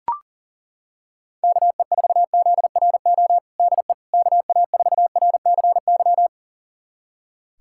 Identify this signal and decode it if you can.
Morse code, heard as a keyed tone, about 40 wpm, 700 Hz. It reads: KE4ZRO DE KA4RCY
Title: USA Callsign DE Callsign 40wpm